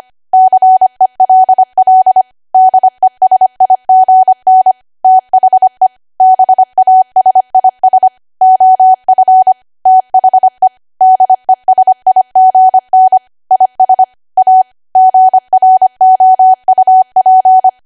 Morse code, heard as a keyed tone, about 25 wpm, 750 Hz. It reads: CELL DESIGN THE BASIS OF THE DESIGN IS A GROUP
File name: cwa / W1AW-Level3 / 25WPM